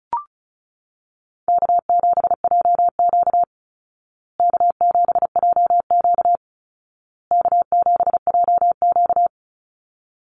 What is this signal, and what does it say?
Morse code, heard as a keyed tone, about 34 wpm, 700 Hz. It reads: K7JQ K7JQ K7JQ
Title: Easy Call Signs Repeated 3x - 35wpm